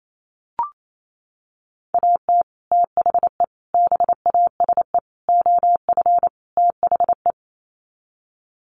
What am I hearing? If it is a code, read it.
Morse code, heard as a keyed tone, about 28 wpm, 700 Hz. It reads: AT THE BASE OF THE
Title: Sets of 5 Words 28wpm